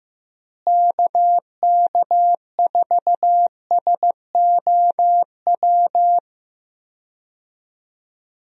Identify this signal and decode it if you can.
Morse code, heard as a keyed tone, about 15 wpm, 700 Hz. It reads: KK4SOW
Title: USA Call Signs - 6 Characters - Repeated 2x - 15wpm